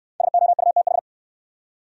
Morse code, heard as a keyed tone, about 69 wpm, 700 Hz. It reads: SOUTH